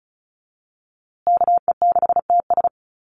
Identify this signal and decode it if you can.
Morse code, heard as a keyed tone, about 34 wpm, 700 Hz. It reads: KE6TS
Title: USA Call Signs Repeated 3x 35wpm